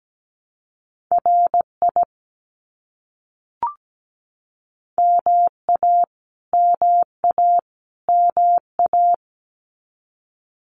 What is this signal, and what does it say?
Morse code, heard as a keyed tone, about 17 wpm, 700 Hz. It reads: RI MA MA MA
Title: US States Repeated 3x 17wpm